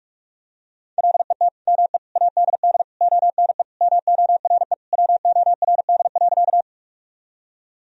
Morse code, heard as a keyed tone, about 46 wpm, 700 Hz. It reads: LET ME ADD ONE MORE WORD.
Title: Sentences from Top 1000 Words - Review Part 1 - 45wpm